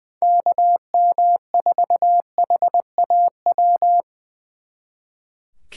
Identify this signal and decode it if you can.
Morse code, heard as a keyed tone, about 24 wpm, 700 Hz. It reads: KM4HAW
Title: Speed Racing - USA Call Signs 20wpm